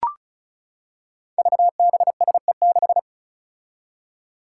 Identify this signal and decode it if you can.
Morse code, heard as a keyed tone, about 34 wpm, 700 Hz. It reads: VBSE6
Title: Five Challenging Letter-Number Combinations 35wpm